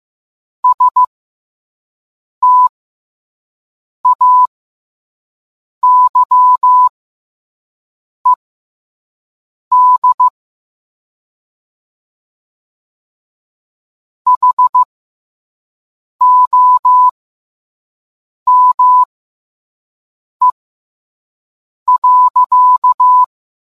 Morse code, heard as a keyed tone, about 15 wpm, spaced as for 5 wpm, 1000 Hz. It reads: STAYED HOME.